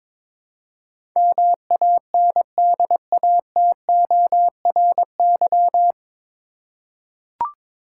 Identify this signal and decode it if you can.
Morse code, heard as a keyed tone, about 22 wpm, 700 Hz. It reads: MANDATORY
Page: Morse Code Ninja practice 22wpm